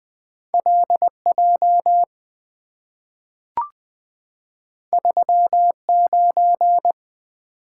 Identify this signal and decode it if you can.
Morse code, heard as a keyed tone, about 20 wpm, 700 Hz. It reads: LJ 39
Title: Two Letter-Number Combinations - 20wpm